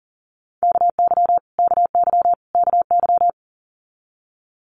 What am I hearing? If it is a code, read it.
Morse code, heard as a keyed tone, about 40 wpm, 700 Hz. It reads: KY KY KY
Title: US States Repeated 3x 40wpm